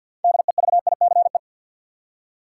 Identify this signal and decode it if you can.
Morse code, heard as a keyed tone, about 53 wpm, 700 Hz. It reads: DEVICE
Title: Six Letter Words - 50wpm